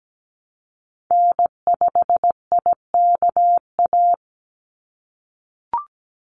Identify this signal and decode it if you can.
Morse code, heard as a keyed tone, about 17 wpm, 700 Hz. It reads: N5IKA
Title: USA Call Signs 17wpm